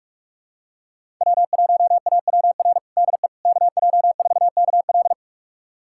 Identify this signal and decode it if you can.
Morse code, heard as a keyed tone, about 46 wpm, 700 Hz. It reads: W1AWR DE KJ4KL
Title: USA Callsign DE Callsign - 5 Characters - 45wpm